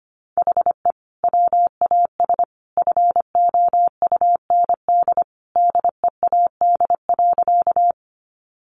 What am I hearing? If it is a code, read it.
Morse code, heard as a keyed tone, about 25 wpm, 700 Hz. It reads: HE WAS FOUND DEAD.
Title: Sentences from Top 1500 Words 25wpm